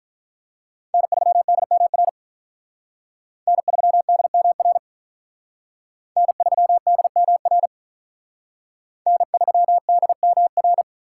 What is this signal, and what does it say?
Morse code, heard as a keyed tone, about 46 wpm, 700 Hz. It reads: N3DMR N3DMR N3DMR N3DMR